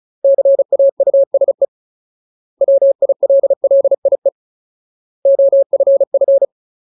Morse code, heard as a keyed tone, about 34 wpm, 550 Hz. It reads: CAUSE WILLIE OFF
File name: Ultimate Word List / 35WPM Ultimate